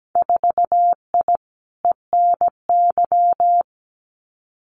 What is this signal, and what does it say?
Morse code, heard as a keyed tone, about 17 wpm, 700 Hz. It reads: 4I ENY
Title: Winter Field Day Exchanges 17wpm